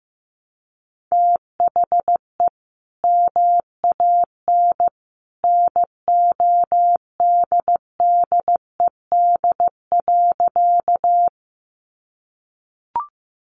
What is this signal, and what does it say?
Morse code, heard as a keyed tone, about 15 wpm, 700 Hz. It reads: THE MAN NODDED.
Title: A Sound Of Thunder 15wpm